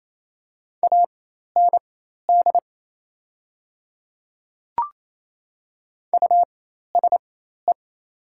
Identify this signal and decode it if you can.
Morse code, heard as a keyed tone, about 28 wpm, spaced as for 13 wpm, 700 Hz. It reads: AND USE
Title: 4x Character Spacing - Top 100 Words - 28wpm